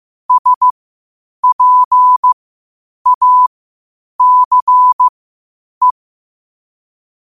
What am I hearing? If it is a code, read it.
Morse code, heard as a keyed tone, about 15 wpm, spaced as for 8 wpm, 1000 Hz. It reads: SPACE
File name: qotdfiles / 8wpm